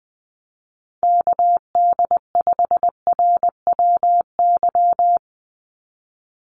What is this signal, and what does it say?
Morse code, heard as a keyed tone, about 20 wpm, 700 Hz. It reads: KD5RWY
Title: USA Call Signs 20wpm